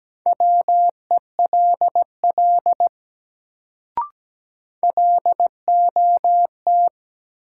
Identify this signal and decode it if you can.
Morse code, heard as a keyed tone, about 17 wpm, 700 Hz. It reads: WELL LOT
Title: Sets of 2 Words - New Word Drill - 17wpm